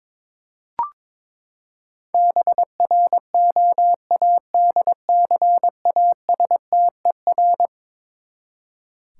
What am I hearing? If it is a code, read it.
Morse code, heard as a keyed tone, about 22 wpm, 700 Hz. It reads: BROADCASTER
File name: practice / 22wpm / Eleven Letter Words - 22wpm